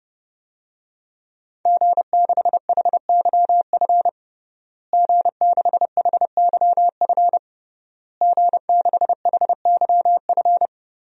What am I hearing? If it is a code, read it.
Morse code, heard as a keyed tone, about 30 wpm, 700 Hz. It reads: G6HYF G6HYF G6HYF
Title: DXCC-100 Call Signs - 5 Characters - Repeated 3x - 30wpm